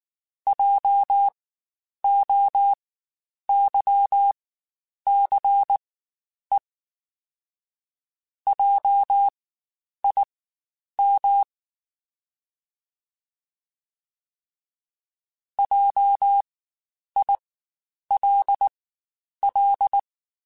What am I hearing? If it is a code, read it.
Morse code, heard as a keyed tone, about 19 wpm, spaced as for 9 wpm, 800 Hz. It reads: JOYCE JIM JILL
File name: Spaced Alphas / Spaced Alphas mp3s / Js sp @18